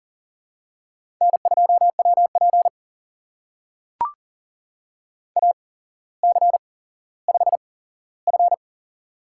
Morse code, heard as a keyed tone, about 40 wpm, spaced as for 17 wpm, 700 Hz. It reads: N2WP AC5F
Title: USA Call Signs - 4 Characters - 8x Character Spacing - 40wpm